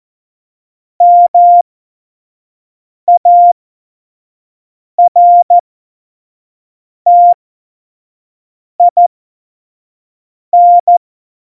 Morse code, heard as a keyed tone, about 14 wpm, spaced as for 5 wpm, 700 Hz. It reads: MARTIN
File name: Morsekurs / Witze / Witz 19. Dezember 2017, 25 F70